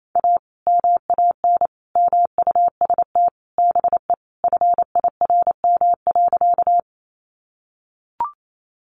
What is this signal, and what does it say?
Morse code, heard as a keyed tone, about 28 wpm, 700 Hz. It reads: A MAN MUST BE FIRM.
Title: Sentences from Top 1400 Words 28wpm